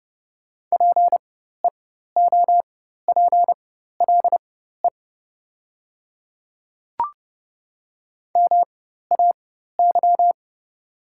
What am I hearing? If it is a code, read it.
Morse code, heard as a keyed tone, about 30 wpm, spaced as for 14 wpm, 700 Hz. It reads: PEOPLE MAY
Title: 4x Character Spacing - Top 100 Words - 30wpm